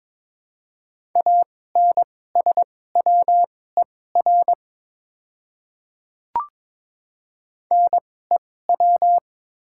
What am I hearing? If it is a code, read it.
Morse code, heard as a keyed tone, about 22 wpm, spaced as for 16 wpm, 700 Hz. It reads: ANSWER NEW
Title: 2x Character Spacing - Top 300 Words - 22wpm